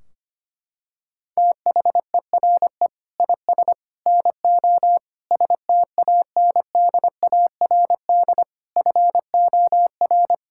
Morse code, heard as a keyed tone, about 25 wpm, 700 Hz. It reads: THERE IS NO STANDARD FOR
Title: Sentences from Top 1600 Words 25wpm